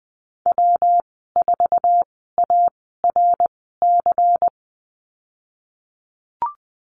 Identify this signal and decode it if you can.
Morse code, heard as a keyed tone, about 20 wpm, spaced as for 15 wpm, 700 Hz. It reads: W4ARC
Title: 231 Lesson 37g - Letter x - Call Signs 20wpm